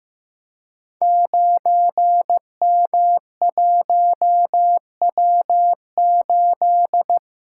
Morse code, heard as a keyed tone, about 15 wpm, 700 Hz. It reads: 9M1W8
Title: Code Groups - 5 Letter:Numbers - 15wpm